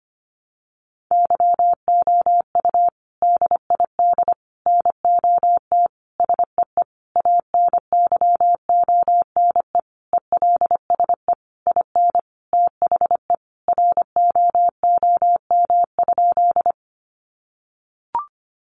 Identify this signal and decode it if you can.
Morse code, heard as a keyed tone, about 25 wpm, 700 Hz. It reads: YOU DID NOT SEE ANYONE ELSE IN THE ROOM?
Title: Sentences from Top 900 Words 25wpm